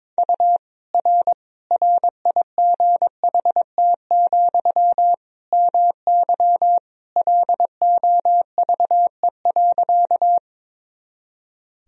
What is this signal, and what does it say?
Morse code, heard as a keyed tone, about 22 wpm, 700 Hz. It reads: U R RIGHT, MY LOVE.
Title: Sentences from Top 400 Words and Abbreviations 22wpm